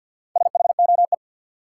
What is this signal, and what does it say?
Morse code, heard as a keyed tone, about 47 wpm, 700 Hz. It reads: SHOE
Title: Four Letter Words - 50wpm